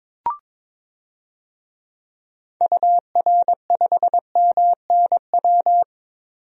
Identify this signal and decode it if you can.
Morse code, heard as a keyed tone, about 22 wpm, 700 Hz. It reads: UR5MNW
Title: DXCC-100 Call Signs - 6 Characters - 22wpm